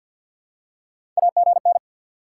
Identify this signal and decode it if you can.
Morse code, heard as a keyed tone, about 53 wpm, 700 Hz. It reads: AGN
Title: Top 90 QSO Elements 50wpm